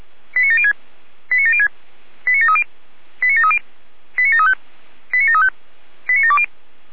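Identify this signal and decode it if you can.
Five-tone selective call (ZVEI), heard as quick runs of five tones, tones of about 70 ms. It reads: 89706 89706 89730 89730 89735 89735 89720
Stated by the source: ZVEI five-tone selective call received with an SDR.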